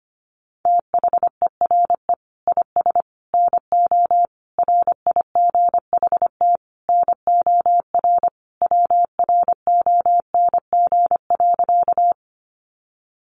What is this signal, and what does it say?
Morse code, heard as a keyed tone, about 25 wpm, 700 Hz. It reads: THERE IS NO RIGHT NOR WRONG.
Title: Sentences from Top 1500 Words 25wpm